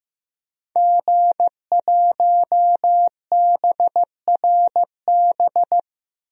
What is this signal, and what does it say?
Morse code, heard as a keyed tone, about 15 wpm, 700 Hz. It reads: G1BRB
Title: DXCC-100 Call Signs - 5 Characters - 15wpm